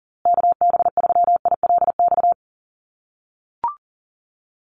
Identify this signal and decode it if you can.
Morse code, heard as a keyed tone, about 40 wpm, 700 Hz. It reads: KB3ILX
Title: USA Call Signs Repeated 3x 40wpm